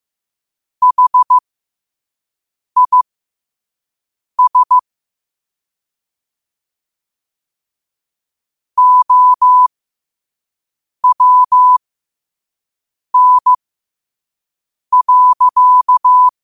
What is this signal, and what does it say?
Morse code, heard as a keyed tone, about 15 wpm, spaced as for 5 wpm, 1000 Hz. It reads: HIS OWN.
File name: qotdfiles / 5wpm